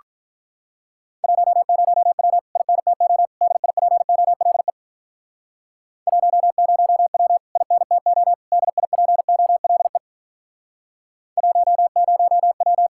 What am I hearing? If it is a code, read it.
Morse code, heard as a keyed tone, about 48 wpm, 700 Hz. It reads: 10W INTO DIPOLE 10W INTO DIPOLE 10W